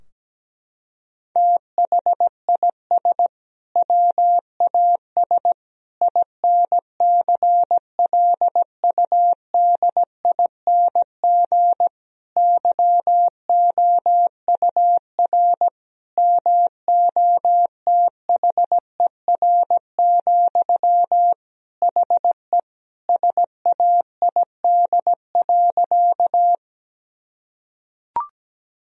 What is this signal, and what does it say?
Morse code, heard as a keyed tone, about 17 wpm, 700 Hz. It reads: THIS WAS INCLUDING YOUR MOTHER, HE SAID.
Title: Sentences from Top 1000 Words - Review Part 4 - 17wpm